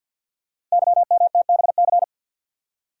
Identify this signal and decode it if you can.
Morse code, heard as a keyed tone, about 50 wpm, 690 Hz. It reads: YMTBC
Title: Code Groups - 5 Letters - 50wpm